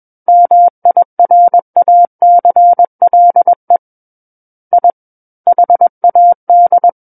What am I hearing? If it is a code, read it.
Morse code, heard as a keyed tone, about 21 wpm, 700 Hz. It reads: MIRACLE I HAD
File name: Book One / 12-Chapter 12 - 21 WPM